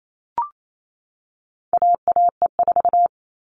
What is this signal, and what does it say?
Morse code, heard as a keyed tone, about 28 wpm, 700 Hz. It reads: AAE4
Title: All Cut Numbers - 4-Digit 28wpm